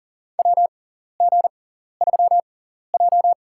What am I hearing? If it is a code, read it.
Morse code, heard as a keyed tone, about 40 wpm, spaced as for 14 wpm, 700 Hz. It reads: WG3J
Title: USA Call Signs - 4 Characters - 6x Character Spacing - 40wpm